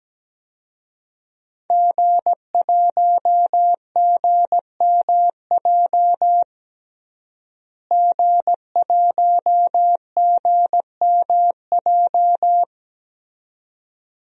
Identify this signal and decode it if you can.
Morse code, heard as a keyed tone, about 17 wpm, 700 Hz. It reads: G1GMJ G1GMJ
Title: DXCC-100 Call Signs - 5 Characters - Repeated 2x - 17wpm